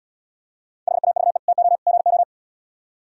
Morse code, heard as a keyed tone, about 76 wpm, 700 Hz. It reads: HAVE TO GO